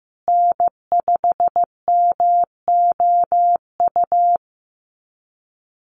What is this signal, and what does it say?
Morse code, heard as a keyed tone, about 15 wpm, 700 Hz. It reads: N5MOU
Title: USA Call Signs Repeated 3x 15wpm